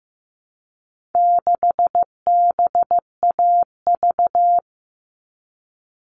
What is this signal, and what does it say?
Morse code, heard as a keyed tone, about 15 wpm, 700 Hz. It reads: 6BAV
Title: All Cut Numbers - 4-Digit 15wpm